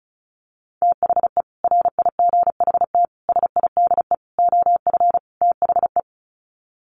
Morse code, heard as a keyed tone, about 34 wpm, 700 Hz. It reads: THE RIGHT SIDE OF THE
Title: Sets of 5 Words Encore 35wpm